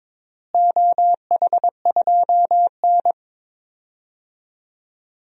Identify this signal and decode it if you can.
Morse code, heard as a keyed tone, about 22 wpm, 700 Hz. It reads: OH2N
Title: DXCC-100 Call Signs - 4 Characters - Repeated 2x - 22wpm